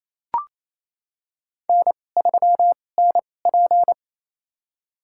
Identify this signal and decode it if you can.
Morse code, heard as a keyed tone, about 28 wpm, spaced as for 20 wpm, 700 Hz. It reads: N3NP
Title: USA Call Signs - 4 Characters - 2x Character Spacing - 28wpm